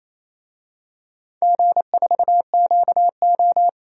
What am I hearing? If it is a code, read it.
Morse code, heard as a keyed tone, about 28 wpm, 700 Hz. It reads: G4QO
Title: DXCC-100 Call Signs - 4 Characters - 28wpm